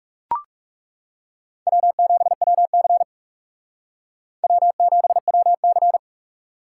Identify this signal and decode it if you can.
Morse code, heard as a keyed tone, about 38 wpm, 700 Hz. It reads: W7WC W7WC